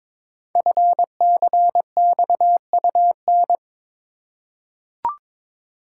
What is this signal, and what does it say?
Morse code, heard as a keyed tone, about 22 wpm, 700 Hz. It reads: FCXUN
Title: Code Groups - 5 Letters - 22wpm